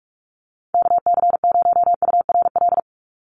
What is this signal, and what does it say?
Morse code, heard as a keyed tone, about 44 wpm, 700 Hz. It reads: KC0URL
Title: Call Signs - 45wpm